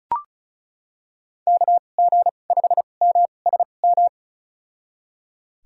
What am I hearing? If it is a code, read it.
Morse code, heard as a keyed tone, about 36 wpm, spaced as for 25 wpm, 700 Hz. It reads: KG5MSM
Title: USA Call Signs - 6 Characters - 2x Character Spacing - 35wpm